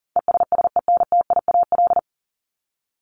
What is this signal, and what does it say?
Morse code, heard as a keyed tone, about 40 wpm, 700 Hz. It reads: ESSENTIAL
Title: LY Suffix - Top 100 Words with Base Words 40wpm